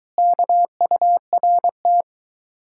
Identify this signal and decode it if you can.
Morse code, heard as a keyed tone, about 23 wpm, 700 Hz. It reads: KURT